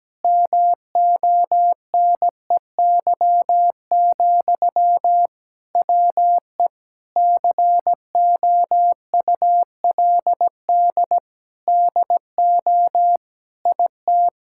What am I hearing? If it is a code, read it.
Morse code, heard as a keyed tone, about 17 wpm, 700 Hz. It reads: MONEY, WE COULD DO IT
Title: Sentences from Top 1600 Words 17wpm